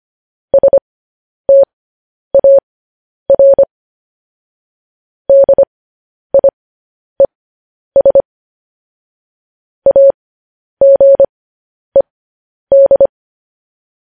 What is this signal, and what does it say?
Morse code, heard as a keyed tone, about 25 wpm, spaced as for 10 wpm, 550 Hz. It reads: STAR DIES AGED
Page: news headlines in morse code at 10 wpm